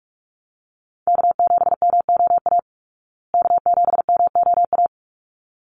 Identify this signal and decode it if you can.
Morse code, heard as a keyed tone, about 44 wpm, 700 Hz. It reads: K7MOA K7MOA
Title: Call Signs Repeated 3x - 45wpm